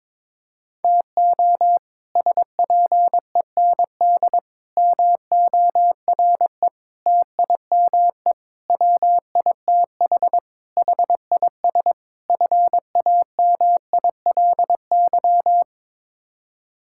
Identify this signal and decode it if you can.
Morse code, heard as a keyed tone, about 22 wpm, 700 Hz. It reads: TO SPEND MORE TIME WITH HIS FAMILY
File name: practice / 22wpm / Sets of 7 Words - 22wpm